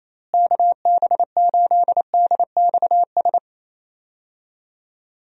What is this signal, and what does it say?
Morse code, heard as a keyed tone, about 27 wpm, 700 Hz. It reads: KB8DXS
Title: USA Call Signs - 6 Characters - Repeated 2x - 28wpm